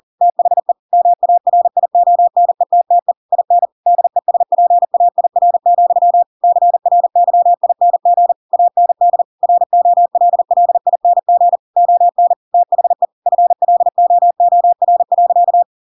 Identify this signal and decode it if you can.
Morse code, heard as a keyed tone, about 40 wpm, 700 Hz. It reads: THE MARIONETTE IN DESPAIR, CRYING AND ROLLING ON THE FLOOR.
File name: Pinocchio / Pinocchio ch7 40 40 ik1hge